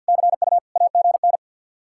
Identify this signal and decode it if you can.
Morse code, heard as a keyed tone, about 65 wpm, spaced as for 57 wpm, 700 Hz. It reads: CU AGN
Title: Speed-Racing Common QSO Phrases - 50wpm